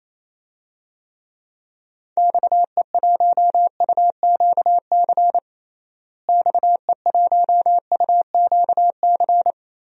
Morse code, heard as a keyed tone, about 28 wpm, 700 Hz. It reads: XE1UQC XE1UQC
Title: DXCC-100 Call Signs - 6 Characters - Repeated 3x - 28wpm